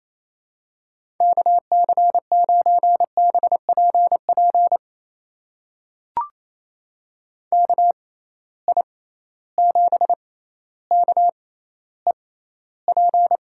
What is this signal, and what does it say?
Morse code, heard as a keyed tone, about 28 wpm, spaced as for 14 wpm, 700 Hz. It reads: KC9BPP KI7KEP
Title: USA Call Signs - 6 Characters - 6x Character Spacing - 28wpm